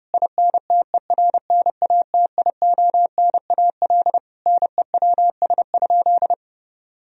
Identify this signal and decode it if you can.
Morse code, heard as a keyed tone, about 32 wpm, 700 Hz. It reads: INTERNATIONAL NEWS?
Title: Sentences from Top 1000 Words - Review Part 4 - 30wpm